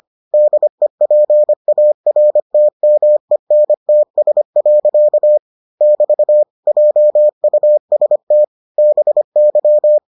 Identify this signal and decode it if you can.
Morse code, heard as a keyed tone, about 25 wpm, 600 Hz. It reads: DEPARTMENTS. <BT> JUST BY